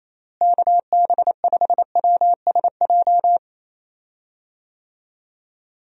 Morse code, heard as a keyed tone, about 28 wpm, 700 Hz. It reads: KB5WSJ
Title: USA Call Signs - 6 Characters - Repeated 3x - 28wpm